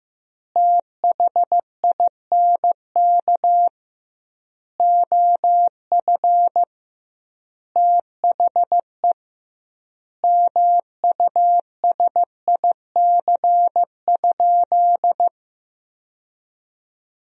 Morse code, heard as a keyed tone, about 15 wpm, 700 Hz. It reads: THINK OF THE MUSIC?
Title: 2x Word-Spacing - Sentences from Top 300 Words - 15wpm